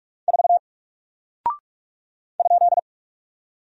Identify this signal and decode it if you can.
Morse code, heard as a keyed tone, about 44 wpm, 700 Hz. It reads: 4 ?
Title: Single Character (Letter:Number:Punctuation:Prosign) 45wpm